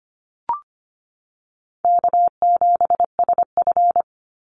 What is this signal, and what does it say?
Morse code, heard as a keyed tone, about 25 wpm, 700 Hz. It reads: K7SF
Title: Easy Call Signs - 25wpm